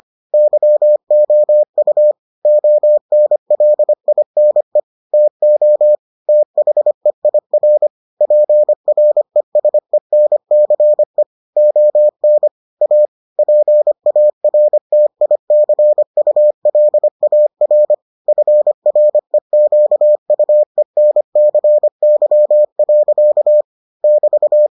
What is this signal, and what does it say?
Morse code, heard as a keyed tone, about 25 wpm, 600 Hz. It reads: YOU ONLINE TO THEIR PRESENCE ON A PARTICULAR FREQUENCY. <BT>